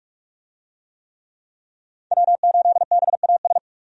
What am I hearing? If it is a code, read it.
Morse code, heard as a keyed tone, about 46 wpm, 700 Hz. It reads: W8BAS
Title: USA Call Signs - 5 Characters - 45wpm